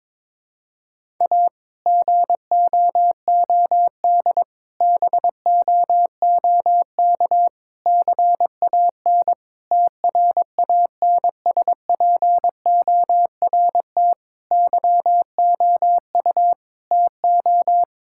Morse code, heard as a keyed tone, about 22 wpm, 700 Hz. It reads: A GOOD BOOK CAN TRANSPORT YOU TO
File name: Difficult Sentences with Hints - 22wpm